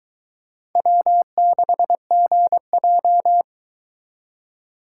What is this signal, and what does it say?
Morse code, heard as a keyed tone, about 23 wpm, 700 Hz. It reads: W6GJ